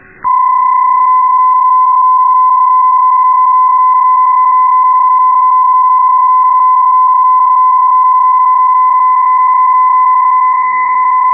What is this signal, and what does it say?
PSK31, about 31.25 baud.